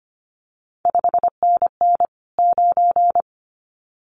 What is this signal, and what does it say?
Morse code, heard as a keyed tone, about 25 wpm, 700 Hz. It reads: 5NN 9
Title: CQ WW DX Contest Exchange - 25wpm